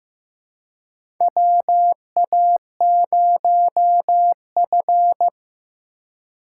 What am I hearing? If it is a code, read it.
Morse code, heard as a keyed tone, about 15 wpm, 700 Hz. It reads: WA0F
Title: USA Call Signs - 4 Characters - 4x Character Spacing - 15wpm